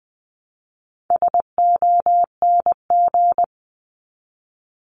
Morse code, heard as a keyed tone, about 20 wpm, 700 Hz. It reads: SONG
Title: Top 500 Words 20wpm